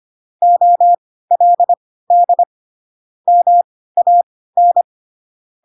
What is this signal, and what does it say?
Morse code, heard as a keyed tone, about 25 wpm, spaced as for 16 wpm, 700 Hz. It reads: OLD MAN